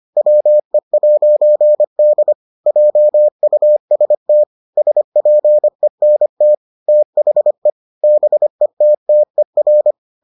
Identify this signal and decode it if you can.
Morse code, heard as a keyed tone, about 25 wpm, 600 Hz. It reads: WE'D JUST SPENT THE BETTER